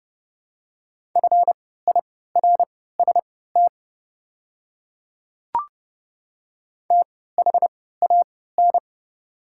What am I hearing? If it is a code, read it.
Morse code, heard as a keyed tone, about 31 wpm, spaced as for 17 wpm, 700 Hz. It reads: FIRST THAN
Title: 3x Character Spacing - Top 100 Words - 30wpm